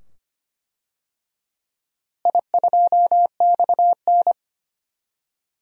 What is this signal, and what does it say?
Morse code, heard as a keyed tone, about 25 wpm, 700 Hz. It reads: I2XN
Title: DXCC-100 Call Signs - 4 Characters - Repeated 2x - 25wpm